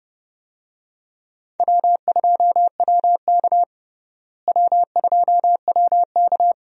Morse code, heard as a keyed tone, about 30 wpm, 700 Hz. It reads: W2WK W2WK
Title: USA Call Signs - 4 Characters - Repeated 3x - 30wpm